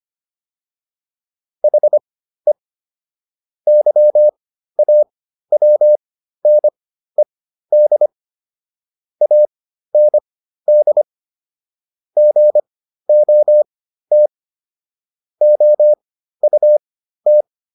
Morse code, heard as a keyed tone, about 25 wpm, spaced as for 13 wpm, 600 Hz. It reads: HE YAWNED AND GOT OUT